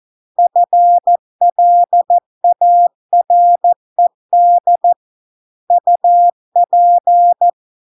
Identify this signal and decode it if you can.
Morse code, heard as a keyed tone, about 14 wpm, 700 Hz. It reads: FLARED UP